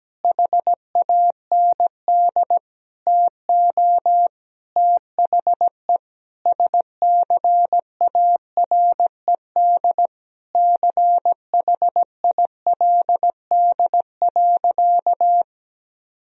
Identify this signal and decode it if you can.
Morse code, heard as a keyed tone, about 17 wpm, 700 Hz. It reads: HAND TO THE SCARED CHILD.
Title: Sentences from Top 1000 Words - Review Part 4 - 17wpm